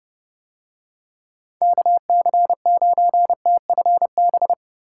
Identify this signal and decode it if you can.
Morse code, heard as a keyed tone, about 29 wpm, 700 Hz. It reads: KC9TFB